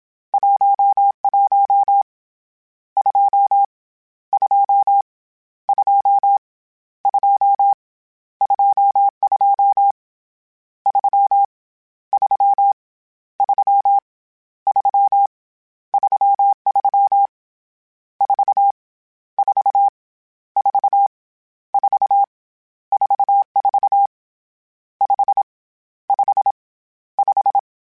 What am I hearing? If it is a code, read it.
Morse code, heard as a keyed tone, about 27 wpm, 800 Hz. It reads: 11 2 2 2 2 22 3 3 3 3 33 4 4 4 4 44 5 5 5
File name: Misc / Misc-mp3s / Numb @25